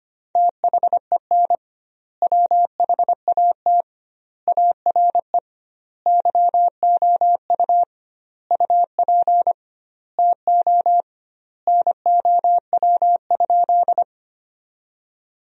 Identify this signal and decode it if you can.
Morse code, heard as a keyed tone, about 25 wpm, 700 Hz. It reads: THEN WHAT ARE YOU UP TO NOW?
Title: 2x Word-Spacing - Sentences from Top 100 Words - 25wpm